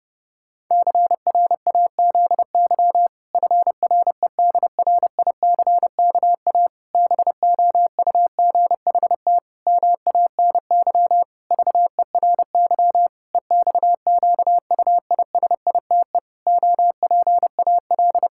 Morse code, heard as a keyed tone, about 30 wpm, 700 Hz. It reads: CRAZY FREDRICKA BOUGHT MANY VERY EXQUISITE OPAL
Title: Pangram Challenge v2 - 30wpm